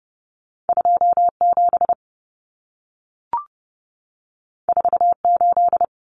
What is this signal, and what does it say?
Morse code, heard as a keyed tone, about 30 wpm, 700 Hz. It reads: 27 48
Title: Common Cut Numbers - 2-Digit 30wpm